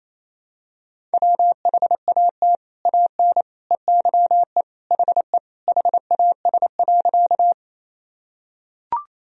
Morse code, heard as a keyed tone, about 28 wpm, 700 Hz. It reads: WHAT AN EYE HE HAS.
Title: Sentences from Top 1000 Words - Review Part 1 - 28wpm